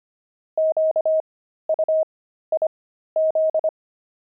Morse code, heard as a keyed tone, about 25 wpm, spaced as for 13 wpm, 630 Hz. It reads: QUIZ
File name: pangram-fs-13-cs-25-wived